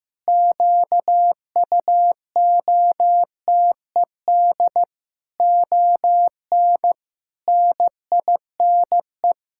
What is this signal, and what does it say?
Morse code, heard as a keyed tone, about 15 wpm, 700 Hz. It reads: QUOTED ON NINE
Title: Pangram Challenge v2 - 15wpm